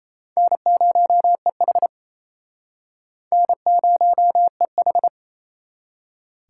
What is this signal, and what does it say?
Morse code, heard as a keyed tone, about 37 wpm, 700 Hz. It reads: N0EH N0EH